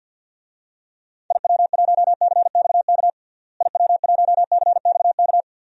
Morse code, heard as a keyed tone, about 48 wpm, 700 Hz. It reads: IW1CXK IW1CXK
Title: DXCC-100 Call Signs - 6 Characters - Repeated 2x - 50wpm